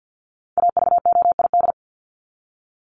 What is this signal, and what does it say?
Morse code, heard as a keyed tone, about 50 wpm, 700 Hz. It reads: AVOID
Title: Top 751-1000 Words 50wpm